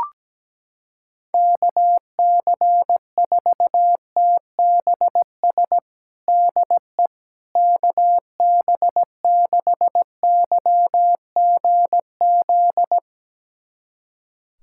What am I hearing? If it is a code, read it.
Morse code, heard as a keyed tone, about 17 wpm, 700 Hz. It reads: KC4TBS DE KB6YGZ
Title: USA Callsign DE Callsign - 6 Characters - 17wpm